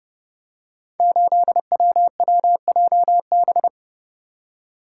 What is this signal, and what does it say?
Morse code, heard as a keyed tone, about 30 wpm, 700 Hz. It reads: 8WWJB
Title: Code Groups - 5 Letter:Numbers - 30wpm